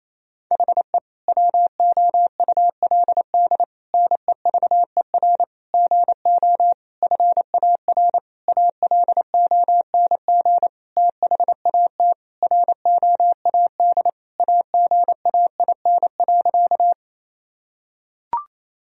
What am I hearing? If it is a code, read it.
Morse code, heard as a keyed tone, about 28 wpm, 700 Hz. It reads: HE WOULD NEVER GO FAR ALONG THAT ROAD AGAIN.
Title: Sentences from Top 1000 Words - Review Part 2 - 28wpm